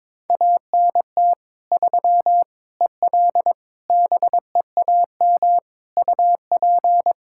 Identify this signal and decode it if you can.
Morse code, heard as a keyed tone, about 22 wpm, 700 Hz. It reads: ANT 3 EL BEAM UP
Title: Practice QSO - Exchange 2 - Info - 22wpm